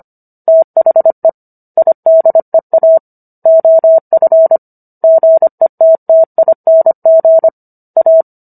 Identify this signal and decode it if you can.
Morse code, heard as a keyed tone, about 25 wpm, 650 Hz. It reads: THE IDEA OF GETTING A